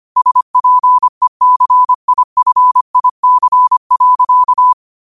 Morse code, heard as a keyed tone, about 25 wpm, 1000 Hz. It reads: SPECIFIC.
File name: qotdfiles / 25wpm